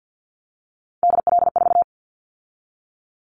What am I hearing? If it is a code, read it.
Morse code, heard as a keyed tone, about 50 wpm, 700 Hz. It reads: DL4
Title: Callsign Prefixes for Top 50 DXCC - 50wpm